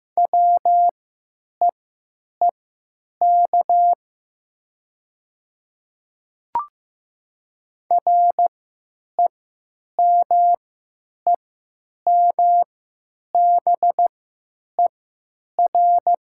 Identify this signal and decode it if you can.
Morse code, heard as a keyed tone, about 15 wpm, spaced as for 9 wpm, 700 Hz. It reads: WEEK REMEMBER
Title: 3x Character Spacing - Top 500 Words - 15wpm